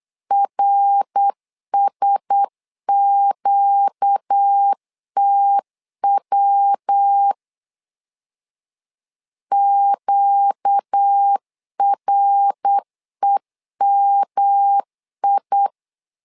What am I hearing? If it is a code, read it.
Morse code, heard as a keyed tone, about 8 wpm, 800 Hz. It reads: RSQTW QREMI
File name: LEZIONI / 4 Q1S 8wpm 2710